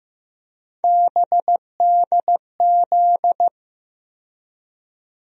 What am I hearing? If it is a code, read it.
Morse code, heard as a keyed tone, about 15 wpm, 700 Hz. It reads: BDZ
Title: USA Call Sign Postfixes - 3 Letters 15wpm